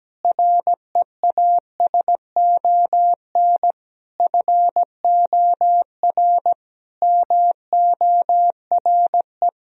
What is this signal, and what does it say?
Morse code, heard as a keyed tone, about 17 wpm, 700 Hz. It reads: REASON FOR MORE
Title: Sentences from Top 1000 Words - Review Part 4 - 17wpm